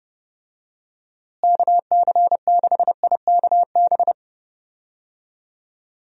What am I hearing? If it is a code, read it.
Morse code, heard as a keyed tone, about 30 wpm, 700 Hz. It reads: KC6IKB
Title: USA Call Signs - 6 Characters - Repeated 3x - 30wpm